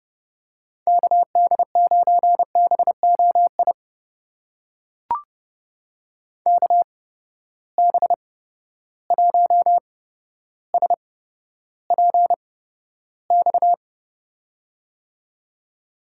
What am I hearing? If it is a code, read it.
Morse code, heard as a keyed tone, about 30 wpm, spaced as for 13 wpm, 700 Hz. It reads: KD9BOI KB1SPX
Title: USA Call Signs - 6 Characters - 8x Character Spacing - 30wpm